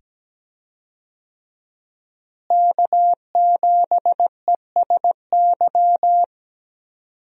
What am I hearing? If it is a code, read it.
Morse code, heard as a keyed tone, about 17 wpm, 700 Hz. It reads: K7ESY
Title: USA Call Signs - 5 Characters - Repeated 2x - 17wpm